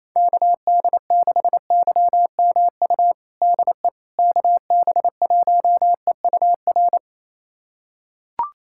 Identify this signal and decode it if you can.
Morse code, heard as a keyed tone, about 27 wpm, 700 Hz. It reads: KD6YMU DE KB1EUR
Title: USA Callsign DE Callsign - 6 Characters - 28wpm